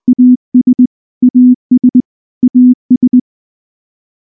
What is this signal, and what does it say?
Morse code, heard as a keyed tone, about 19 wpm, 250 Hz. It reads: AS AS AS
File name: morse / morse-training-20WPM-6